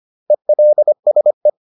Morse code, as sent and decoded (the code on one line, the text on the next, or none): . .-.. ... .
ELSE